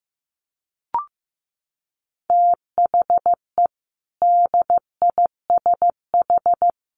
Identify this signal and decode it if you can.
Morse code, heard as a keyed tone, about 15 wpm, 700 Hz. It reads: THE DISH
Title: Sentences from Top 1400 Words 15wpm